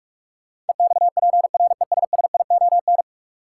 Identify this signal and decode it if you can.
Morse code, heard as a keyed tone, about 46 wpm, 700 Hz. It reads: EXPRESSION
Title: Sentences from Top 1200 Words - New Word Drill - 45wpm